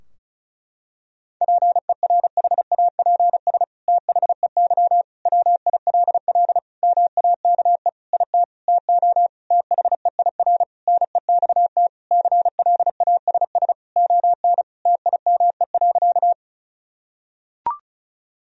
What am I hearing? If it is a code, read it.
Morse code, heard as a keyed tone, about 36 wpm, 700 Hz. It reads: PERHAPS THEY WILL MAKE IT TO THEIR NEXT CLASS ON TIME.
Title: Sentences from Top 1000 Words - Review Part 2 - 35wpm